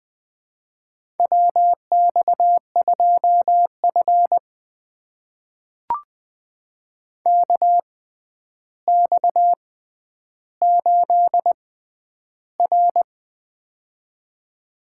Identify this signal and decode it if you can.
Morse code, heard as a keyed tone, about 20 wpm, spaced as for 10 wpm, 700 Hz. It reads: WX2F KX8R
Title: USA Call Signs - 4 Characters - 6x Character Spacing - 20wpm